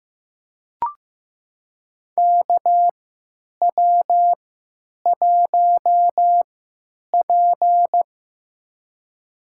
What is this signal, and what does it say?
Morse code, heard as a keyed tone, about 15 wpm, spaced as for 9 wpm, 700 Hz. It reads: KW1P
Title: USA Call Signs - 4 Characters - 3x Character Spacing - 15wpm